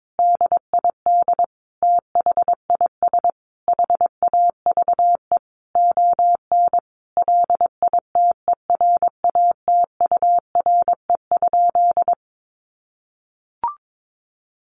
Morse code, heard as a keyed tone, about 22 wpm, 700 Hz. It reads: DID THIS HAVE ON LITERATURE?
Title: Sentences from Top 1000 Words 22wpm